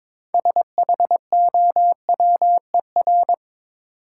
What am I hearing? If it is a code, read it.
Morse code, heard as a keyed tone, about 22 wpm, 700 Hz. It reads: SHOWER
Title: Sentences from Top 1600 Words - New Word Drill - 22wpm